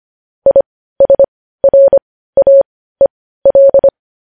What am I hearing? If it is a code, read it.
Morse code, heard as a keyed tone, about 25 wpm, spaced as for 15 wpm, 550 Hz. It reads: ISRAEL
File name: podcast / News headlines-1x-25x15